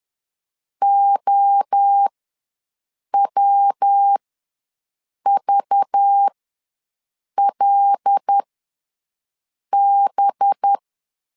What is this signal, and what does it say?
Morse code, heard as a keyed tone, about 10 wpm, spaced as for 6 wpm, 800 Hz. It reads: OWVLB
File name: LEZIONI / 7 GF9Z / 7 GF ws 10wpm 3918